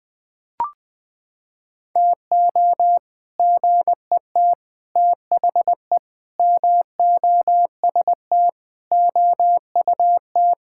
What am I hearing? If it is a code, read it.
Morse code, heard as a keyed tone, about 21 wpm, 700 Hz. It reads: TO GET THE MOST OUT
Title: Sentences with 7-Word Phrases - 20wpm